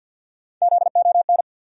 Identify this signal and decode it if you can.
Morse code, heard as a keyed tone, about 47 wpm, 700 Hz. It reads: ZON
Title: USA Call Sign Postfixes - 3 Letters 50wpm